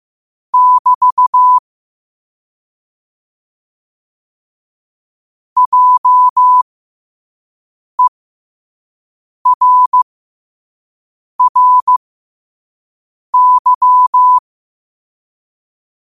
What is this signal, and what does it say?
Morse code, heard as a keyed tone, about 15 wpm, spaced as for 5 wpm, 1000 Hz. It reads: <BT> JERRY
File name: qotdfiles / 5wpm